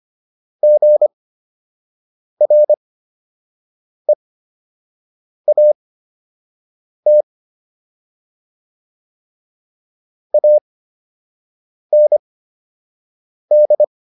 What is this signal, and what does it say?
Morse code, heard as a keyed tone, about 25 wpm, spaced as for 6 wpm, 600 Hz. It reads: GREAT AND